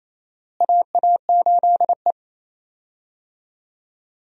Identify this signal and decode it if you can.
Morse code, heard as a keyed tone, about 28 wpm, 700 Hz. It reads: AA8E
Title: USA Call Signs - 4 Characters - Repeated 2x - 28wpm